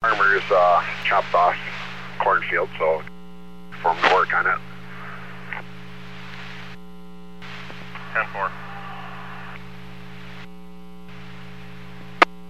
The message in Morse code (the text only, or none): none